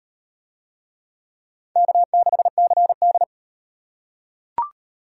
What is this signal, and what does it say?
Morse code, heard as a keyed tone, about 38 wpm, 700 Hz. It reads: K6CD